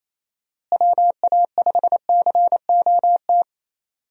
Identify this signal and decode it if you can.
Morse code, heard as a keyed tone, about 28 wpm, 700 Hz. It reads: WA5COT
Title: USA Call Signs - 6 Characters - Repeated 3x - 28wpm